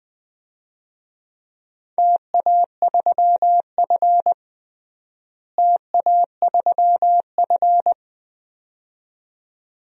Morse code, heard as a keyed tone, about 21 wpm, 700 Hz. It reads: TA3F TA3F
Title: DXCC-100 Call Signs - 4 Characters - Repeated 2x - 20wpm